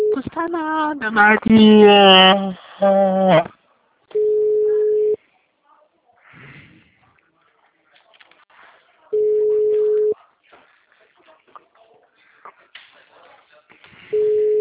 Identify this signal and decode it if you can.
Ringback tone (Europe, 425 Hz), with speech.
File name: out- -2007- - - .52326